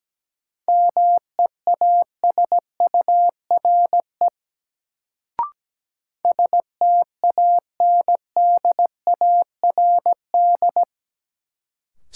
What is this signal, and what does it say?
Morse code, heard as a keyed tone, about 17 wpm, 700 Hz. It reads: MEASURE STANDARD
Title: Sentences from Top 1200 Words - New Word Drill Alt - 17wpm